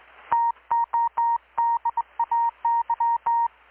Morse code, heard as a keyed tone, about 20 wpm, 930 Hz. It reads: TODAY
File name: blog / swl / 6-abril-2019-12695-USB-KPS